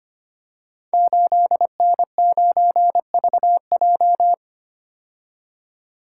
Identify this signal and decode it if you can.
Morse code, heard as a keyed tone, about 25 wpm, 700 Hz. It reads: 8N9VJ
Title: Code Groups - 5 Letter:Numbers - 25wpm